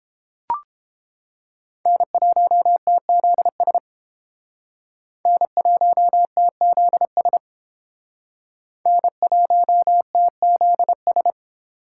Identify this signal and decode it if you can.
Morse code, heard as a keyed tone, about 30 wpm, 700 Hz. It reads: N1TZS N1TZS N1TZS